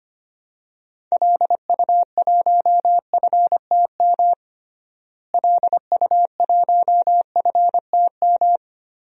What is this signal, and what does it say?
Morse code, heard as a keyed tone, about 25 wpm, 700 Hz. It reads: LU1FTM LU1FTM
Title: DXCC-100 Call Signs - 6 Characters - Repeated 3x - 25wpm